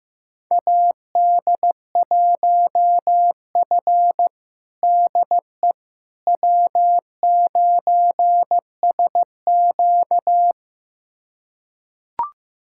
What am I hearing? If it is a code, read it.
Morse code, heard as a keyed tone, about 15 wpm, 700 Hz. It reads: AD1F DE W9SQ
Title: USA Callsign DE Callsign - 4 Characters - 15wpm